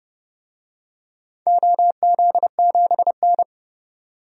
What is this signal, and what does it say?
Morse code, heard as a keyed tone, about 30 wpm, 700 Hz. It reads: OZ7N
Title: DXCC-100 Call Signs - 4 Characters - 30wpm